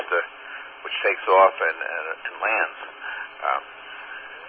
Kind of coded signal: Emergency Alert System (SAME) tones